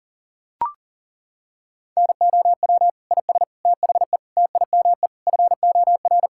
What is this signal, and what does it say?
Morse code, heard as a keyed tone, about 40 wpm, 700 Hz. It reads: NOW IS THE TIME FOR
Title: Pangram Challenge v2 - 40wpm